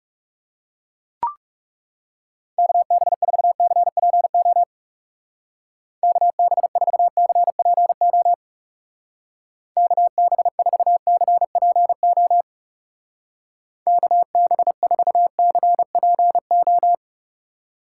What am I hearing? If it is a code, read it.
Morse code, heard as a keyed tone, about 36 wpm, 700 Hz. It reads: KB4CPO KB4CPO KB4CPO KB4CPO
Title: Speed Racing - USA 2x3 Call Signs 30wpm